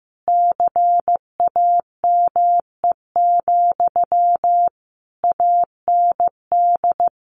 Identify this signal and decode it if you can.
Morse code, heard as a keyed tone, about 15 wpm, 700 Hz. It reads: CAME, AND
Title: Sentences from Top 1500 Words 15wpm